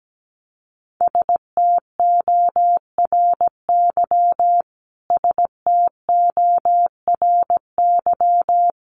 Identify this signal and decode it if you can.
Morse code, heard as a keyed tone, about 17 wpm, 700 Hz. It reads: STORY STORY
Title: Top 500 Words Repeated 17wpm